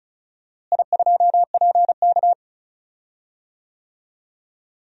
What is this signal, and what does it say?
Morse code, heard as a keyed tone, about 36 wpm, 700 Hz. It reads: I2PK
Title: DXCC-100 Call Signs - 4 Characters - Repeated 3x - 35wpm